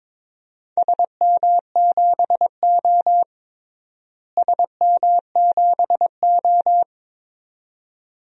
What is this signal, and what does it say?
Morse code, heard as a keyed tone, about 22 wpm, 700 Hz. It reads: SM7O SM7O
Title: DXCC-100 Call Signs - 4 Characters - Repeated 2x - 22wpm